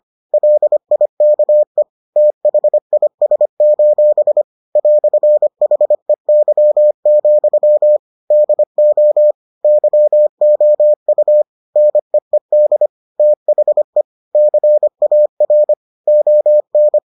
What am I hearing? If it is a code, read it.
Morse code, heard as a keyed tone, about 25 wpm, 600 Hz. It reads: LIKE THIS: "HEY, DO YOU NEED THE CAR ON